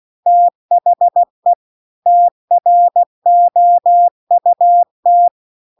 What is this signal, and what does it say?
Morse code, heard as a keyed tone, about 16 wpm, 700 Hz. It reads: THE TROUT